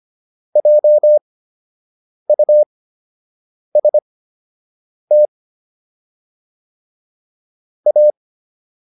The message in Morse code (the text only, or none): JUST A